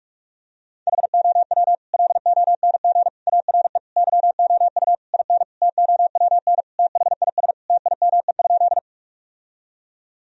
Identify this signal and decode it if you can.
Morse code, heard as a keyed tone, about 42 wpm, 700 Hz. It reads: HOW LONG ARE YOU IN TOWN THIS TIME?
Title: Sentences from Top 1000 Words - Review Part 2 - 45wpm